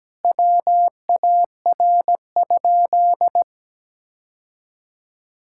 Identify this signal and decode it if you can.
Morse code, heard as a keyed tone, about 17 wpm, 700 Hz. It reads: WAR?
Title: Sentences from Top 1600 Words 17wpm